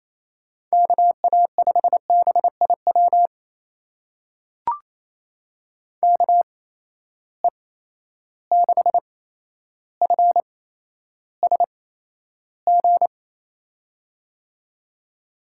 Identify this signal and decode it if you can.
Morse code, heard as a keyed tone, about 28 wpm, spaced as for 12 wpm, 700 Hz. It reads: KA5BIW KE6FSG
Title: USA Call Signs - 6 Characters - 8x Character Spacing - 28wpm